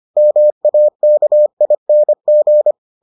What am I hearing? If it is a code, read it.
Morse code, heard as a keyed tone, about 25 wpm, 600 Hz. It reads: MAKING